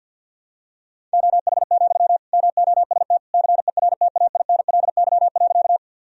Morse code, heard as a keyed tone, about 48 wpm, 700 Hz. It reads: OH, MOST CERTAINLY.
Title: Sentences from Top 1000 Words - Review Part 1 - 50wpm